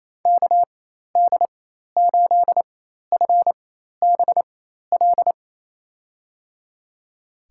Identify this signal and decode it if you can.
Morse code, heard as a keyed tone, about 27 wpm, spaced as for 13 wpm, 700 Hz. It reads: KD8FBL
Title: USA Call Signs - 6 Characters - 4x Character Spacing - 28wpm